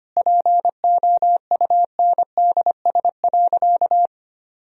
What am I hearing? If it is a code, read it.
Morse code, heard as a keyed tone, about 25 wpm, 700 Hz. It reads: POUNDS.
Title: Sentences from Top 1000 Words - Review Part 2 - 25wpm